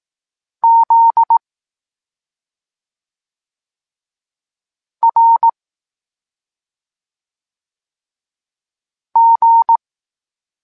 Morse code, heard as a keyed tone, about 18 wpm, 920 Hz. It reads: Z R G
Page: Tekst 23.3 – Ciąg dalszy doskonalenia odbioru poznanych liter